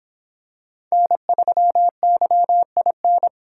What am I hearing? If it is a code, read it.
Morse code, heard as a keyed tone, about 26 wpm, 700 Hz. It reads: N3YIN